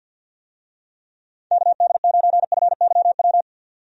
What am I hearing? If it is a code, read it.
Morse code, heard as a keyed tone, about 51 wpm, 700 Hz. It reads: KD9FYW
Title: USA Call Signs - 6 Characters - 50wpm